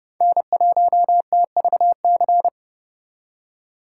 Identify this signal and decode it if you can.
Morse code, heard as a keyed tone, about 38 wpm, 700 Hz. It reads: N1TVC